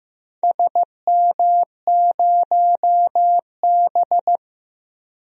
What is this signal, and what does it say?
Morse code, heard as a keyed tone, about 15 wpm, 700 Hz. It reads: SM0B
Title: DXCC-100 Call Signs - 4 Characters - 15wpm